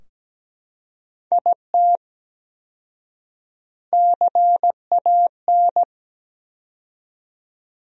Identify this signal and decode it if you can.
Morse code, heard as a keyed tone, about 17 wpm, 700 Hz. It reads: IT CAN
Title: 4x Word-Spacing - Sentences from Top 400 Words - 17wpm